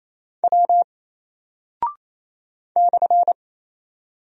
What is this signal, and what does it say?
Morse code, heard as a keyed tone, about 27 wpm, 700 Hz. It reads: W /
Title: Single Character (Letter:Number:Punctuation:Prosign) 28wpm